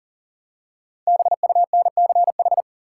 Morse code, heard as a keyed tone, about 40 wpm, 700 Hz. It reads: BUNCH
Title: Five Letter Words - 40wpm